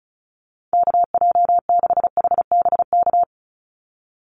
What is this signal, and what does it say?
Morse code, heard as a keyed tone, about 34 wpm, 700 Hz. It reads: KJ6HBK
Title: USA Call Signs 35wpm